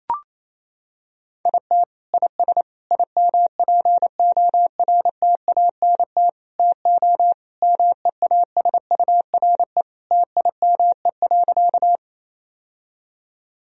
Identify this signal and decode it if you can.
Morse code, heard as a keyed tone, about 28 wpm, 700 Hz. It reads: IT IS IMPORTANT TO MEASURE TIME.
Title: Sentences from Top 1000 Words - Review Part 2 - 28wpm